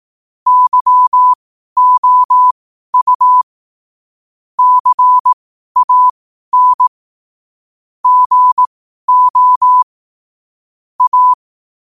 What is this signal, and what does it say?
Morse code, heard as a keyed tone, about 18 wpm, spaced as for 12 wpm, 1000 Hz. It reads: YOU CAN GO A
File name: qotdfiles / 12wpm